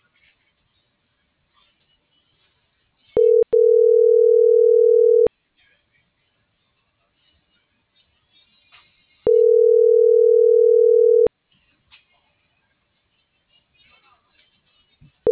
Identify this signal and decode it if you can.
Ringback tone (North America), 440 + 480 Hz.